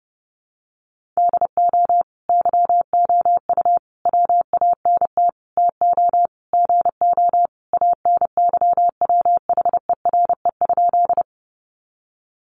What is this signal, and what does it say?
Morse code, heard as a keyed tone, about 30 wpm, 700 Hz. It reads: DO YOU WANT TO GO ANYWHERE?
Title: Sentences from Top 1400 Words 30wpm